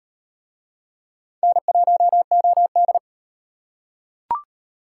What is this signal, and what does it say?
Morse code, heard as a keyed tone, about 38 wpm, 700 Hz. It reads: N1OD